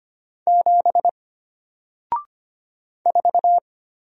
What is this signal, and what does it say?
Morse code, heard as a keyed tone, about 25 wpm, 700 Hz. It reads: 7 4
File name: Single-Letter-Number-25wpm